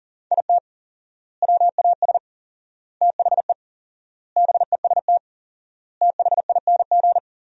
Morse code, heard as a keyed tone, about 37 wpm, 700 Hz. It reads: IT WAS THE BEST THING
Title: 4x Word-Spacing - Sentences from Top 400 Words - 40wpm